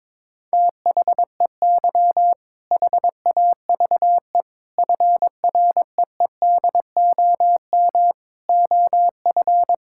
Morse code, heard as a keyed tone, about 22 wpm, 700 Hz. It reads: THEY HAVE FREEDOM OF
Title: Sentences from Top 1000 Words - Review Part 5 - 22wpm